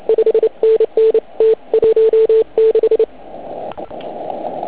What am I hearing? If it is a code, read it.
Morse code, heard as a keyed tone, about 29 wpm, 440 Hz. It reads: 5NN T16
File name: Archiv / CUC149 / OK1IF CUC149